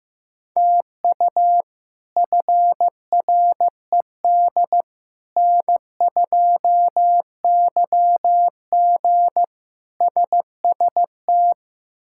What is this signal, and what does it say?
Morse code, heard as a keyed tone, about 15 wpm, 700 Hz. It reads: TU FRED N2YG SST
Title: K1USN SST QSO Practice - 15wpm